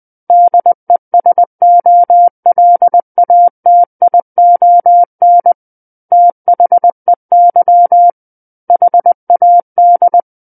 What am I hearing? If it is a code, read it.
Morse code, heard as a keyed tone, about 20 wpm, 700 Hz. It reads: DESOLATION THEY HAD
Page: SKCC Learning Center - Morse Code Practice Files